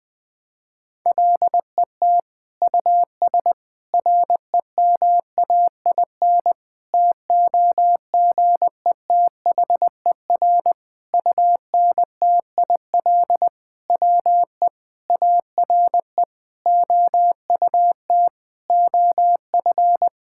Morse code, heard as a keyed tone, about 20 wpm, 700 Hz. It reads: LET US REMAIN TOGETHER UNTIL WE ARE OUT OF
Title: Sentences from Top 1600 Words 20wpm